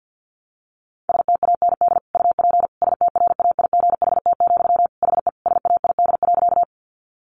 Morse code, heard as a keyed tone, about 50 wpm, 700 Hz. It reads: STAND UP STRAIGHT, HE SAID.
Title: Sentences from Top 1000 Words 50wpm